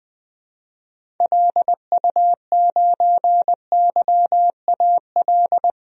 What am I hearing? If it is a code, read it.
Morse code, heard as a keyed tone, about 19 wpm, 700 Hz. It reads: LU9YAL
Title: DXCC-100 Call Signs - 6 Characters - Repeated 3x - 20wpm